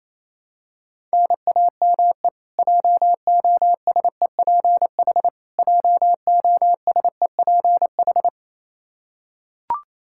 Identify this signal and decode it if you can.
Morse code, heard as a keyed tone, about 28 wpm, 700 Hz. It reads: NAME JOSEPH JOSEPH
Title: Practice QSO - Exchange 1 - Beginning - 28wpm